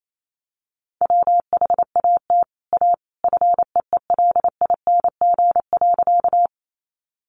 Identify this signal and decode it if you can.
Morse code, heard as a keyed tone, about 28 wpm, 700 Hz. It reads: WHAT A FEELING.
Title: Sentences from Top 1200 Words 28wpm